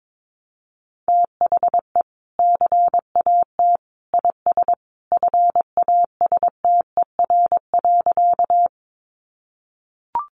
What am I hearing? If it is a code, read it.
Morse code, heard as a keyed tone, about 22 wpm, 700 Hz. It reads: THE CAT IS FASTER.
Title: Additional Easy Phrases 22wpm